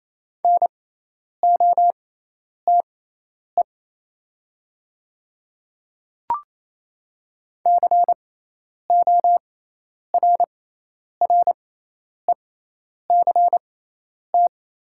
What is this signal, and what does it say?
Morse code, heard as a keyed tone, about 28 wpm, spaced as for 10 wpm, 700 Hz. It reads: NOTE CORRECT
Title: 6x Character Spacing - Top 500 Words - 28wpm